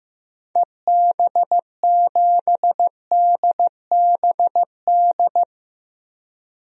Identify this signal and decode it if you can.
Morse code, heard as a keyed tone, about 15 wpm, 700 Hz. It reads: EB7DBD
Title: DXCC-100 Call Signs - 6 Characters - Repeated 3x - 15wpm